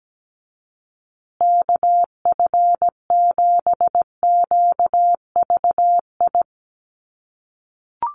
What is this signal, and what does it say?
Morse code, heard as a keyed tone, about 17 wpm, 700 Hz. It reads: KF7QVI
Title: USA Call Signs Repeated 3x 17wpm